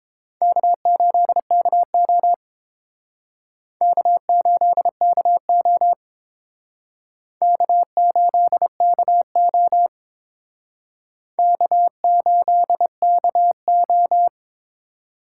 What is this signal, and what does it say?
Morse code, heard as a keyed tone, about 28 wpm, 700 Hz. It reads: K8KO K8KO K8KO K8KO